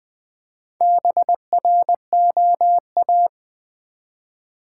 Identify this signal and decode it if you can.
Morse code, heard as a keyed tone, about 20 wpm, 700 Hz. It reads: BROA
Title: Build A Word - 1 Letter At A Time - 3 to 5-Letter Words - 20wpm